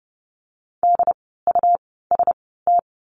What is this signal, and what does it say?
Morse code, heard as a keyed tone, about 30 wpm, spaced as for 17 wpm, 700 Hz. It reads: DUST
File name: Lesson 17f - Letter u - Words 30wpm